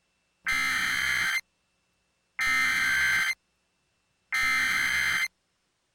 Emergency Alert System (SAME) tones (header bursts), heard as harsh buzzing bursts. It is heard from 0.4 to 5.3 s.